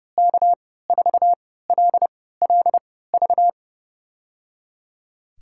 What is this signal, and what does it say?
Morse code, heard as a keyed tone, about 28 wpm, spaced as for 17 wpm, 690 Hz. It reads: K4LLV
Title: USA Call Signs - 3x Character Spacing - 30wpm